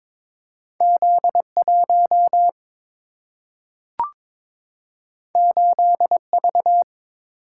Morse code, heard as a keyed tone, about 22 wpm, 700 Hz. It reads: Z1 8V
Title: Single-Letter Single-Number 22wpm